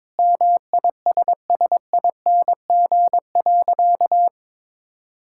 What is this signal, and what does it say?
Morse code, heard as a keyed tone, about 22 wpm, 700 Hz. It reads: MISSING.
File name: practice / 22wpm / Sentences with 4-Word Phrases - 22wpm